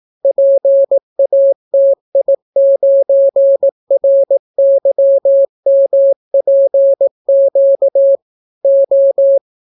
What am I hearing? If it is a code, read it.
Morse code, heard as a keyed tone, about 18 wpm, 550 Hz. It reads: PATI9RYMPQ O